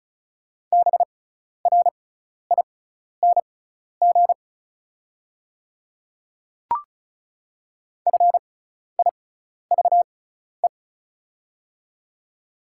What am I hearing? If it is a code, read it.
Morse code, heard as a keyed tone, about 36 wpm, spaced as for 12 wpm, 700 Hz. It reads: BRING FIVE
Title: 6x Character Spacing - Top 500 Words - 35wpm